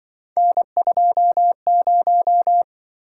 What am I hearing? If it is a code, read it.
Morse code, heard as a keyed tone, about 24 wpm, 700 Hz. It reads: N20
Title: Speed-Racing - Three Letter-Number Combinations - 20wpm